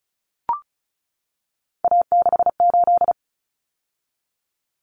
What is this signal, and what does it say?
Morse code, heard as a keyed tone, about 34 wpm, 700 Hz. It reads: A68
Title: Common Cut Numbers - 3-Digit 35wpm